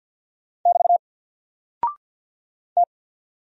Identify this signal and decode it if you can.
Morse code, heard as a keyed tone, about 48 wpm, 700 Hz. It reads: <BT> T
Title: Single Character (Letter:Number:Punctuation:Prosign) Alt 50wpm